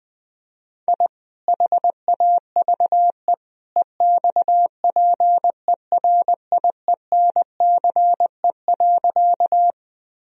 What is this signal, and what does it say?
Morse code, heard as a keyed tone, about 20 wpm, 700 Hz. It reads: I HAVE EXPERIENCE.
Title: Sentences from Top 1000 Words - Review Part 3 - 20wpm